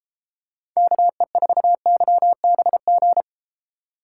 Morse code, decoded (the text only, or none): KE4YBG